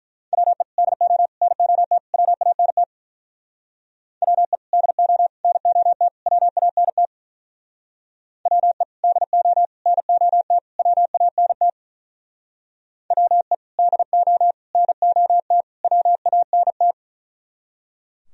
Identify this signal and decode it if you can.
Morse code, heard as a keyed tone, about 42 wpm, 700 Hz. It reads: WE DO NOT WANT WE DO NOT WANT WE DO NOT WANT WE DO NOT WANT
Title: Speed-Racing - Sets of 4 Words 35wpm